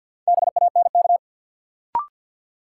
Morse code, heard as a keyed tone, about 51 wpm, 700 Hz. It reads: BANK
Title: Sets of 5 Words - New Word Drill - 50wpm